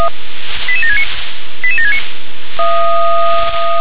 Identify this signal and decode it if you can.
Five-tone selective call (ZVEI), heard as quick runs of five tones, tones of about 70 ms. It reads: 88769 88769